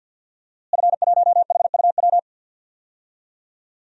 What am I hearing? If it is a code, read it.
Morse code, heard as a keyed tone, about 52 wpm, 700 Hz. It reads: F1HUW
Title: DXCC-100 Call Signs - 5 Characters - Repeated 3x - 50wpm